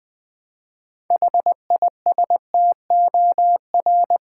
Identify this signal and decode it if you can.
Morse code, heard as a keyed tone, about 20 wpm, 700 Hz. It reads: HISTOR
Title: Build A Word - 2 Letters At A Time - 8-Letter Words - 20wpm